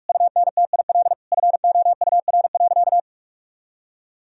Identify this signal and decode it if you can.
Morse code, heard as a keyed tone, about 46 wpm, 700 Hz. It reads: UNTIL FOUR.
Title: Sentences from Top 1000 Words - Review Part 1 - 45wpm